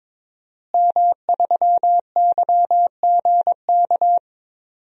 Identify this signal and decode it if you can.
Morse code, heard as a keyed tone, about 22 wpm, 700 Hz. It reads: M3YGK
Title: DXCC-100 Call Signs - 5 Characters - Repeated 2x - 22wpm